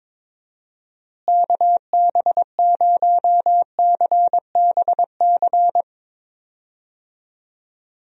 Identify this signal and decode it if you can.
Morse code, heard as a keyed tone, about 22 wpm, 700 Hz. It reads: KB0CBC
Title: USA Call Signs - 6 Characters - Repeated 2x - 22wpm